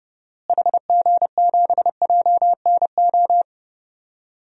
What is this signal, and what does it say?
Morse code, heard as a keyed tone, about 32 wpm, 700 Hz. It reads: HG7JNO
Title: DXCC-100 Call Signs - 6 Characters - Repeated 2x - 30wpm